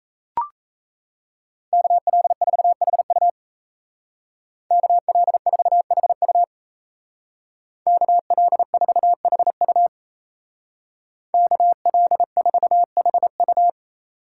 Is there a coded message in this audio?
Yes: Morse code from 1.7 to 13.7 s, heard as a keyed tone, about 36 wpm, 700 Hz. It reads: KL4HU KL4HU KL4HU KL4HU